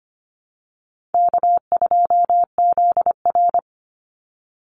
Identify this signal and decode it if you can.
Morse code, heard as a keyed tone, about 25 wpm, 700 Hz. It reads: K2ZR
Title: Call Signs Repeated 2x - 25wpm